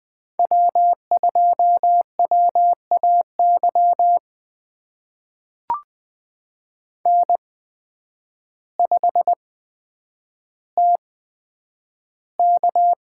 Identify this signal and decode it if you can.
Morse code, heard as a keyed tone, about 20 wpm, spaced as for 9 wpm, 700 Hz. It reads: W2WAY N5TK
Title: USA Call Signs - 8x Character Spacing - 20wpm